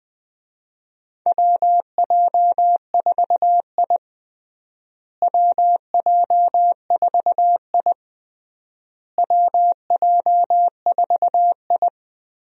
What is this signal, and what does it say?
Morse code, heard as a keyed tone, about 21 wpm, 700 Hz. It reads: WJ4I WJ4I WJ4I
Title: USA Call Signs - 4 Characters - Repeated 3x - 20wpm